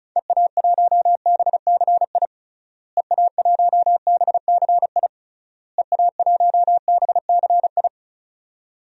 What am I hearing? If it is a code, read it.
Morse code, heard as a keyed tone, about 34 wpm, 700 Hz. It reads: EA1BCI EA1BCI EA1BCI
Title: DXCC-100 Call Signs - 6 Characters - Repeated 3x - 35wpm